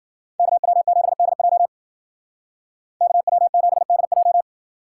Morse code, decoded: KW7DJ KW7DJ

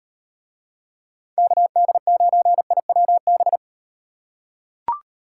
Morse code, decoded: KD9IWB

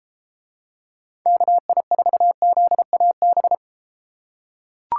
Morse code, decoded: KI4ZAB

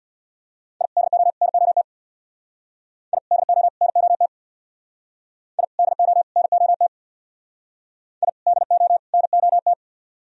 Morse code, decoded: I DO NOT I DO NOT I DO NOT I DO NOT

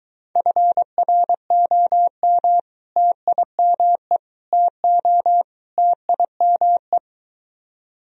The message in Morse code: FROM TIME TO TIME

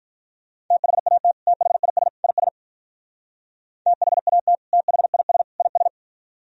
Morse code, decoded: THAT THIS IS THAT THIS IS

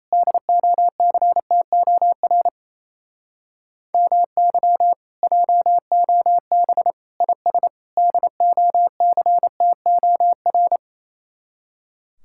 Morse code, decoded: DOCTOR MY JOB IS DOCTOR

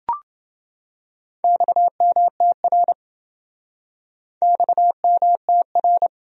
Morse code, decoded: XMTR XMTR

